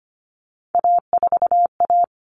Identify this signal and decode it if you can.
Morse code, heard as a keyed tone, about 25 wpm, 700 Hz. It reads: A4A
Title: All Cut Numbers - 3-Digit 25wpm